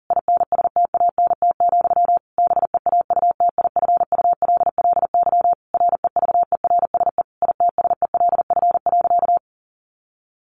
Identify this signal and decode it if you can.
Morse code, heard as a keyed tone, about 40 wpm, 700 Hz. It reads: INSTANT, BEAUTIFULLY REVERSE ITSELF.
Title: A Sound Of Thunder 40wpm